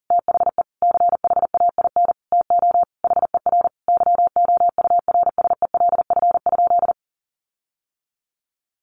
Morse code, heard as a keyed tone, about 40 wpm, 700 Hz. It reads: THE CHAIN TO HER YOURSELF?
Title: Sentences from Top 1500 Words 40wpm